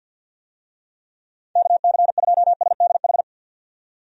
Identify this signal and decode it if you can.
Morse code, heard as a keyed tone, about 51 wpm, 700 Hz. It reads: KC2SDH
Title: USA Call Signs - 6 Characters - 50wpm